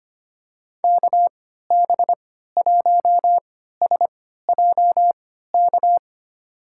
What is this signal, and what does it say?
Morse code, heard as a keyed tone, about 25 wpm, spaced as for 14 wpm, 700 Hz. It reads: KB1SJK
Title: USA Call Signs - 6 Characters - 3x Character Spacing - 25wpm